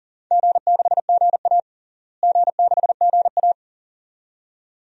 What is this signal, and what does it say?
Morse code, heard as a keyed tone, about 40 wpm, 700 Hz. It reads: G6GA G6GA
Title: DXCC-100 Call Signs - 4 Characters - Repeated 2x - 40wpm